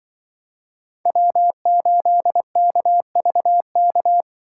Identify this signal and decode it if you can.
Morse code, heard as a keyed tone, about 24 wpm, 700 Hz. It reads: W8KVK
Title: Speed Racing - USA 1x3 Call Signs 20wpm